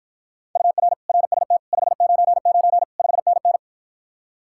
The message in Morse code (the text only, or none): UR RST 599 5NN